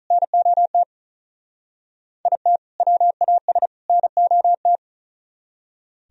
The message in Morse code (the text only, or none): NOT IT WAS NOT